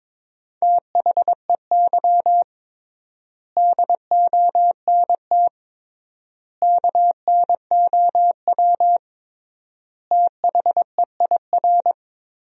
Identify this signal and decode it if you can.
Morse code, heard as a keyed tone, about 22 wpm, 700 Hz. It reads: THEY DONT KNOW THEIR
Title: 3x Word-Spacing - Sentences from Top 300 Words - 22wpm